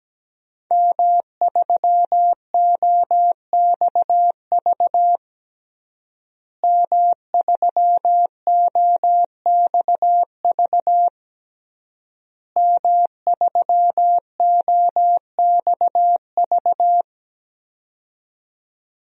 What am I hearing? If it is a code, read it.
Morse code, heard as a keyed tone, about 17 wpm, 700 Hz. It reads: M3OXV M3OXV M3OXV
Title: DXCC-100 Call Signs - 5 Characters - Repeated 3x - 17wpm